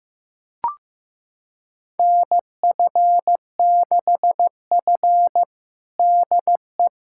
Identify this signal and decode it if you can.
Morse code, heard as a keyed tone, about 15 wpm, 700 Hz. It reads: NF6F DE
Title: USA Callsign DE Callsign - 4 Characters - 15wpm